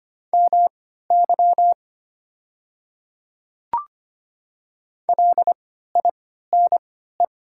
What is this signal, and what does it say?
Morse code, heard as a keyed tone, about 25 wpm, spaced as for 14 wpm, 700 Hz. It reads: MY LINE
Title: 3x Character Spacing - Top 200 Words - 25wpm